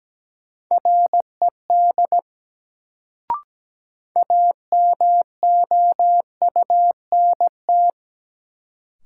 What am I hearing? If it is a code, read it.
Morse code, heard as a keyed tone, about 17 wpm, 700 Hz. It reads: RED AMOUNT
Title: Sentences from Top 700 Words - New Word Drill - 17wpm